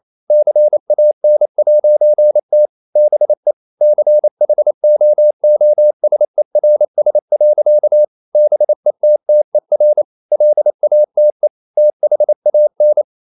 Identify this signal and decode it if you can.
Morse code, heard as a keyed tone, about 28 wpm, 600 Hz. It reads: CAN'T BE CHOOSERS. BETTER LATE THAN
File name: CW / Practice / EnglishSayings28